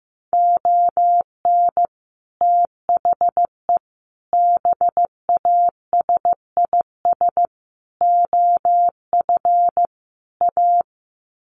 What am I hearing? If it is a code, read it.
Morse code, heard as a keyed tone, about 15 wpm, 700 Hz. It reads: ON THE BASIS OF A
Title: Sets of 5 Words Encore 15wpm